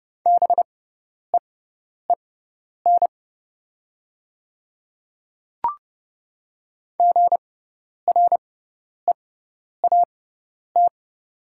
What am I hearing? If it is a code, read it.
Morse code, heard as a keyed tone, about 30 wpm, spaced as for 10 wpm, 690 Hz. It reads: BEEN GREAT
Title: 6x Character Spacing - Top 300 Words - 30wpm